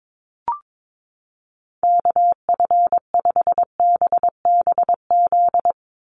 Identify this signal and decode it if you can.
Morse code, heard as a keyed tone, about 22 wpm, 700 Hz. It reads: KF5BBZ
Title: USA Call Signs 22wpm